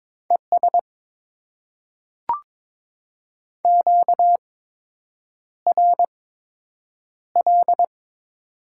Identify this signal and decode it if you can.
Morse code, heard as a keyed tone, about 22 wpm, spaced as for 8 wpm, 700 Hz. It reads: ES QRL
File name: Top 20 QSO Elements - 8x Character Spacing - 22wpm